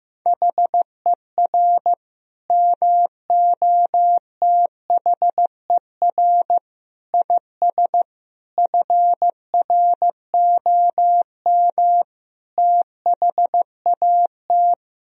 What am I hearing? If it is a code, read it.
Morse code, heard as a keyed tone, about 15 wpm, 700 Hz. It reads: HER MOTHER IS FROM THAT
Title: Sentences from Top 1000 Words - Review Part 2 - 15wpm